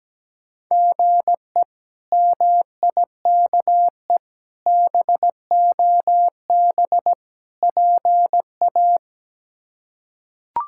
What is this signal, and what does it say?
Morse code, heard as a keyed tone, about 17 wpm, 700 Hz. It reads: GE MIKE BOB PA
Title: K1USN SST QSO Practice - 17wpm